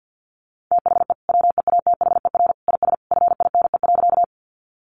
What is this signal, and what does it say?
Morse code, heard as a keyed tone, about 50 wpm, 700 Hz. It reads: THE WEATHER IS FINE.
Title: Sentences from Top 800 Words 50wpm